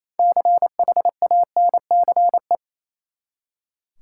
Morse code, heard as a keyed tone, about 28 wpm, 700 Hz. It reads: CHANCE
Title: Six Letter Words - 28wpm